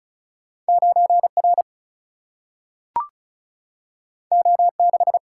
Morse code, heard as a keyed tone, about 36 wpm, 700 Hz. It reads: 9 R O 6
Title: Single-Letter Single-Number 35wpm